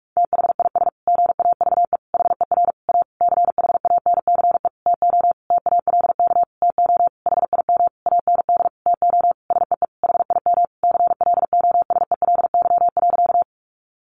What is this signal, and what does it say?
Morse code, heard as a keyed tone, about 45 wpm, 700 Hz. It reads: THIS GAVE HER A CHANCE TO TALK TO HIM AND TO SEE HIM CLOSELY.
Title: Sentences from Top 1400 Words 45wpm